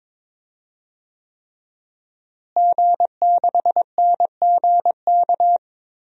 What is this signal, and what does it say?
Morse code, heard as a keyed tone, about 23 wpm, 700 Hz. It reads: G6NGK
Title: DXCC-100 Call Signs - 5 Characters - Repeated 3x - 22wpm